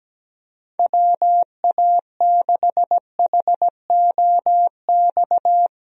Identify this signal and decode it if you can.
Morse code, heard as a keyed tone, about 17 wpm, 700 Hz. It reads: WA6HOX
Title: USA Call Signs - 6 Characters - Repeated 2x - 17wpm